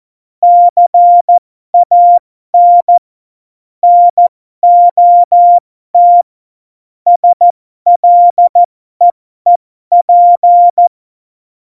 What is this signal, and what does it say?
Morse code, heard as a keyed tone, about 14 wpm, spaced as for 12 wpm, 700 Hz. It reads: CAN NOT SLEEP
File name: Morsekurs / Witze